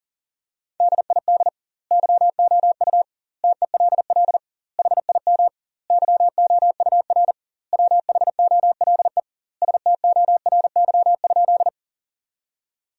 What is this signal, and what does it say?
Morse code, heard as a keyed tone, about 40 wpm, 700 Hz. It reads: DID YOU TELL HIM YOUR WHOLE STORY?
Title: 2x Word-Spacing - Sentences from Top 400 Words - 40wpm